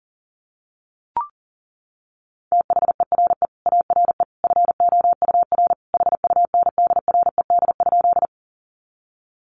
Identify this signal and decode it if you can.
Morse code, heard as a keyed tone, about 40 wpm, 700 Hz. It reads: THERE ARE FOUR HUNDRED?
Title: Sentences from Top 400 Words 40wpm